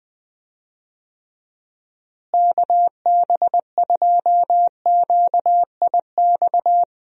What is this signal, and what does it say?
Morse code, heard as a keyed tone, about 20 wpm, 700 Hz. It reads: KB2QIX
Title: USA Call Signs - 6 Characters - Repeated 3x - 20wpm